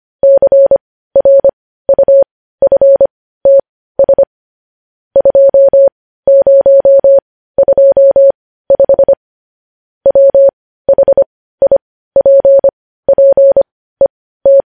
Morse code, heard as a keyed tone, about 25 wpm, spaced as for 15 wpm, 550 Hz. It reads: CRUFTS 2025 WHIPPET
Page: news headlines in morse code at 15 wpm